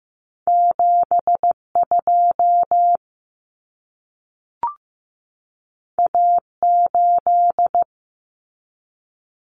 Morse code, heard as a keyed tone, about 15 wpm, 700 Hz. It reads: 72 A8
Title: Common Cut Numbers - 2-Digit 15wpm